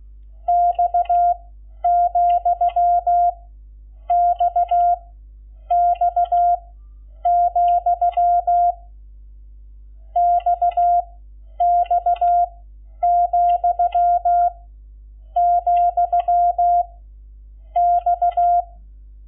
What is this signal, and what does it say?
Morse code, heard as a keyed tone, about 16 wpm, 700 Hz. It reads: X , X X , X X , , X
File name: Lecciones / CLASE Nº11